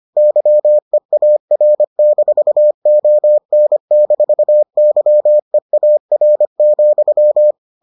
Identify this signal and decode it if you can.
Morse code, heard as a keyed tone, about 25 wpm, 600 Hz. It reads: YEAR-ON-YEAR,